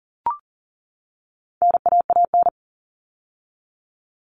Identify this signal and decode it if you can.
Morse code, heard as a keyed tone, about 40 wpm, 700 Hz. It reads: NAAN
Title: All Cut Numbers - 4-Digit 40wpm